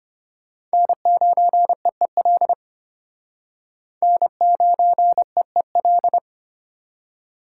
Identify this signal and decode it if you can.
Morse code, heard as a keyed tone, about 33 wpm, 700 Hz. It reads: N9EEL N9EEL